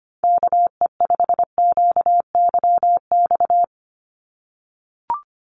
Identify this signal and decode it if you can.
Morse code, heard as a keyed tone, about 25 wpm, 700 Hz. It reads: KE5QYX
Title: USA Call Signs 25wpm